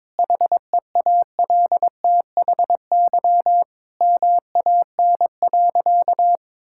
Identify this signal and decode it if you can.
Morse code, heard as a keyed tone, about 22 wpm, 700 Hz. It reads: HEALTHY MAN.
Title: Sentences from Top 1000 Words - Review Part 5 - 22wpm